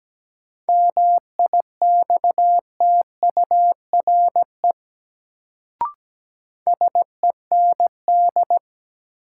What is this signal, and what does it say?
Morse code, heard as a keyed tone, about 17 wpm, 700 Hz. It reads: MIXTURE SEND
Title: Sentences from Top 1500 Words - New Word Drill Alt - 17wpm